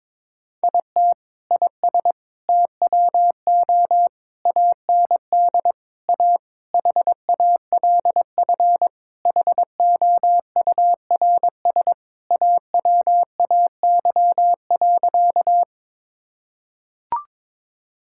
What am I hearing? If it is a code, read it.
Morse code, heard as a keyed tone, about 22 wpm, 700 Hz. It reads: IT IS TWO AND A HALF HOURS AWAY.
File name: practice / 22wpm / Sentences with 4-Word Phrases - 22wpm